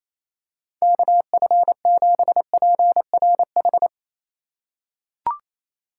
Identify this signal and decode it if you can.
Morse code, heard as a keyed tone, about 28 wpm, 700 Hz. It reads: KF7PRH
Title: USA Call Signs - 4x Character Spacing - 28wpm